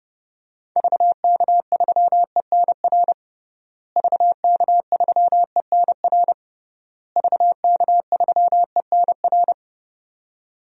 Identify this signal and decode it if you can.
Morse code, heard as a keyed tone, about 31 wpm, 700 Hz. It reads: VK3ENR VK3ENR VK3ENR
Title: DXCC-100 Call Signs - 6 Characters - Repeated 3x - 30wpm